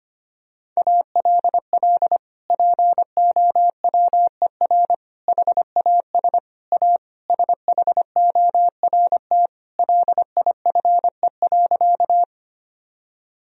Morse code, heard as a keyed tone, about 25 wpm, 700 Hz. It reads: ALL POWER HAS A SHORT LIFE.
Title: Sentences from Top 1000 Words - Review Part 2 - 25wpm